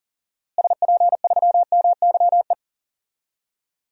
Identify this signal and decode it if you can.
Morse code, heard as a keyed tone, about 40 wpm, 700 Hz. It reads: SP3MYE
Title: DXCC-100 Call Signs - 6 Characters - Repeated 3x - 40wpm